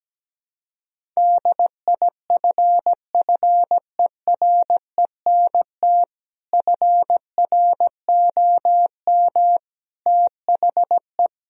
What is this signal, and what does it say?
Morse code, heard as a keyed tone, about 17 wpm, 700 Hz. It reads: DIFFERENT FROM THE
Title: Sentences from Top 1000 Words - Review Part 3 - 17wpm